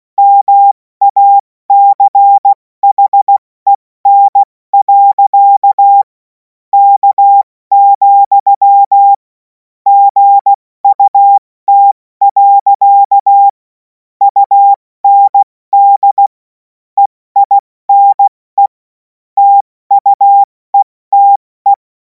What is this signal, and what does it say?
Morse code, heard as a keyed tone, about 16 wpm, spaced as for 14 wpm, 800 Hz. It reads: MACHEN. K, GUT. UND EINE TUETE